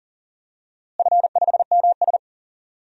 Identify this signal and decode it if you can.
Morse code, heard as a keyed tone, about 38 wpm, 700 Hz. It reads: F5MS
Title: DXCC-100 Call Signs - 4 Characters - Repeated 3x - 40wpm